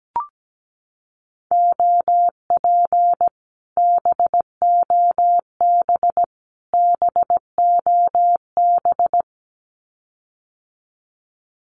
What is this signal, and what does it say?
Morse code, heard as a keyed tone, about 17 wpm, 700 Hz. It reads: OP BOB BOB
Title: Common QSO Phrases Encore 17wpm